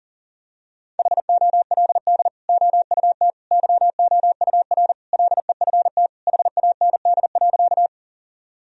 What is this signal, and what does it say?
Morse code, heard as a keyed tone, about 40 wpm, 700 Hz. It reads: HOLD OUT YOUR LEFT HAND.
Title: Sentences from Top 1000 Words - Review Part 2 - 40wpm